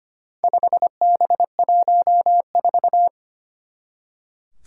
Morse code, heard as a keyed tone, about 25 wpm, 700 Hz. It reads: 5B14
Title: Four Challenging Letter-Number Combinations 25wpm